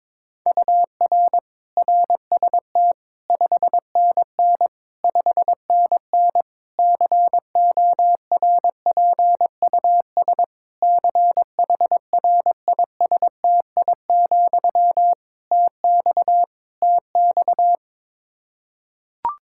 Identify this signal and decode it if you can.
Morse code, heard as a keyed tone, about 22 wpm, 700 Hz. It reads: UR RST 5NN 5NN CORPUS CHRISTI, TX TX
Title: K1USN SST QSO Practice 22wpm